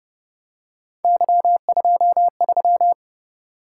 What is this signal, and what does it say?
Morse code, heard as a keyed tone, about 30 wpm, 700 Hz. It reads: Y23